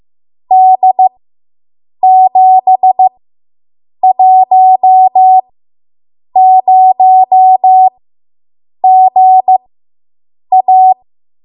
Morse code, heard as a keyed tone, about 15 wpm, spaced as for 7 wpm, 750 Hz. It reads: D710GA